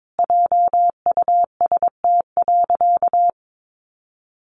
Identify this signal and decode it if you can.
Morse code, heard as a keyed tone, about 22 wpm, 700 Hz. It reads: JUST.
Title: Sentences from Top 1500 Words 22wpm